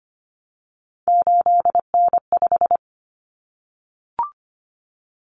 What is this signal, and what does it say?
Morse code, heard as a keyed tone, about 25 wpm, 700 Hz. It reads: 8N5
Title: Common Cut Numbers - 3-Digit 25wpm